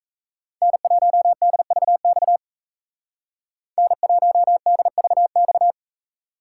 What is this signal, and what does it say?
Morse code, heard as a keyed tone, about 36 wpm, 700 Hz. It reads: N1DVX N1DVX